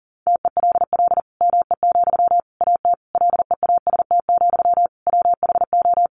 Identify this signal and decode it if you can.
Morse code, heard as a keyed tone, about 40 wpm, 700 Hz. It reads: TELL ME, AT LEAST, WHO
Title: Sentences from Top 700 Words 40wpm